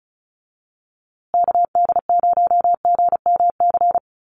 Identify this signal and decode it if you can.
Morse code, heard as a keyed tone, about 34 wpm, 700 Hz. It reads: KD0GMC
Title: USA Call Signs Repeated 3x 35wpm